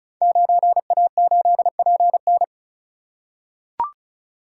Morse code, heard as a keyed tone, about 36 wpm, 690 Hz. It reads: 9A8PN
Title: Code Groups - 5 Letter:Numbers - 35wpm